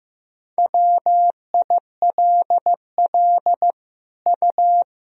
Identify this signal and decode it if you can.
Morse code, heard as a keyed tone, about 16 wpm, 700 Hz. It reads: WILL U
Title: Sentences from Top 200 Words and Abbreviations 15wpm